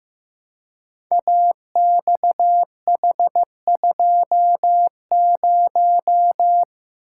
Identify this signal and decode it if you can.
Morse code, heard as a keyed tone, about 15 wpm, 700 Hz. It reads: AXH20
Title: Code Groups - 5 Letter:Numbers - 15wpm